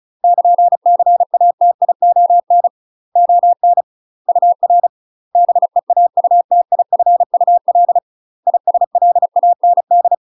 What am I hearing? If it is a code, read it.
Morse code, heard as a keyed tone, about 34 wpm, 700 Hz. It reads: <KN>CATION ON UR BEAUTIFUL ISLAND
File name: UK MASTER QSOs TRAINING/35WPM